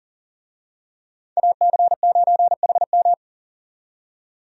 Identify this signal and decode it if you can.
Morse code, heard as a keyed tone, about 37 wpm, 700 Hz. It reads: AC9HM
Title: USA Call Signs - 5 Characters - 40wpm